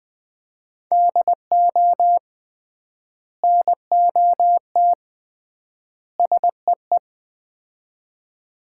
Morse code, heard as a keyed tone, about 20 wpm, 700 Hz. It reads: DO NOT SEE
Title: 3x Word-Spacing - Sets of 3 Words - 20wpm